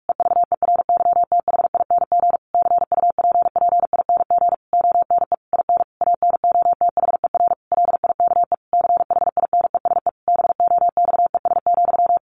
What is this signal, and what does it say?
Morse code, heard as a keyed tone, about 44 wpm, 700 Hz. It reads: EVERYTHING CUPPING ONE IN ANOTHER LIKE CHINESE BOXES,
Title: A Sound Of Thunder 45wpm